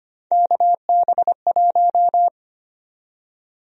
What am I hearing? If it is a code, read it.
Morse code, heard as a keyed tone, about 25 wpm, 700 Hz. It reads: KB1
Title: USA Call Sign Prefixes - 25wpm